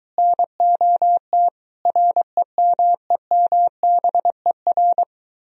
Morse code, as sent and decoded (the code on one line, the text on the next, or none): -. --- - / .-. -- . -- -... .-.
NOT REMEMBER